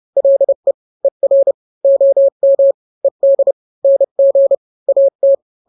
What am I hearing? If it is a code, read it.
Morse code, heard as a keyed tone, about 30 wpm, 550 Hz. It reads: LE ER OM ED NG AT
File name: n-grams / 2-grams 30 25 WPM